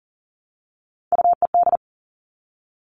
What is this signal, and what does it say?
Morse code, heard as a keyed tone, about 38 wpm, 690 Hz. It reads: UED
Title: Three Challenging Letter-Number Combinations 40wpm